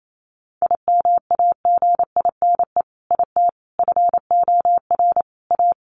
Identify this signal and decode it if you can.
Morse code, heard as a keyed tone, about 28 wpm, 700 Hz. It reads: IMAGINE IT FOR A
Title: Sentences from Top 1400 Words 28wpm